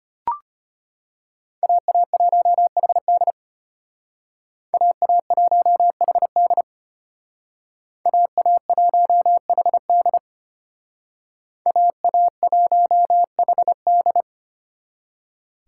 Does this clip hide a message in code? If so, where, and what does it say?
Yes: Morse code from 1.6 to 14.2 s, heard as a keyed tone, about 33 wpm, 700 Hz. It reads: AA1HD AA1HD AA1HD AA1HD